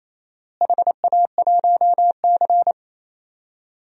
Morse code, heard as a keyed tone, about 28 wpm, 700 Hz. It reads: HA1C
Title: DXCC-100 Call Signs - 4 Characters - Repeated 3x - 28wpm